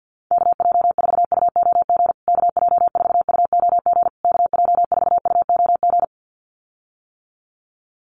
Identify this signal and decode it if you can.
Morse code, heard as a keyed tone, about 50 wpm, 700 Hz. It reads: KJ4UOG KJ4UOG KJ4UOG
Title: USA Call Signs Repeated 3x 50wpm